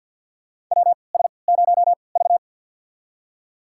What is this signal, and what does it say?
Morse code, heard as a keyed tone, about 52 wpm, spaced as for 29 wpm, 700 Hz. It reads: WS0V
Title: USA Call Signs - 3x Character Spacing - 50wpm